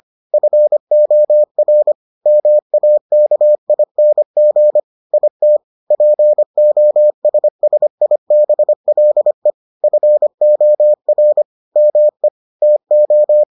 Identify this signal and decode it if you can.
Morse code, heard as a keyed tone, about 25 wpm, 600 Hz. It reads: FOR MAKING IT POSSIBLE FOR ME TO